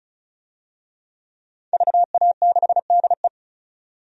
Morse code, heard as a keyed tone, about 35 wpm, 700 Hz. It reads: VA6DE
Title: DXCC-100 Call Signs - 5 Characters - 35wpm